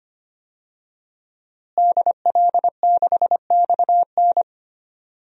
Morse code, heard as a keyed tone, about 25 wpm, 700 Hz. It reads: DL6XN
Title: DXCC-100 Call Signs - 5 Characters - Repeated 2x - 25wpm